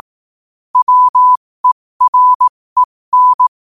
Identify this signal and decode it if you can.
Morse code, heard as a keyed tone, about 18 wpm, spaced as for 15 wpm, 1000 Hz. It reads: WEREN
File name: qotdfiles / 15wpm